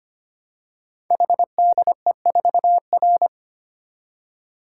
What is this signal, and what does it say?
Morse code, heard as a keyed tone, about 25 wpm, 700 Hz. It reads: HDE4R
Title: Code Groups - 5 Letter:Numbers - 25wpm